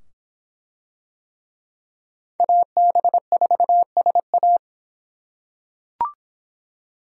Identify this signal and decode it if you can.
Morse code, heard as a keyed tone, about 23 wpm, 700 Hz. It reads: AB4SA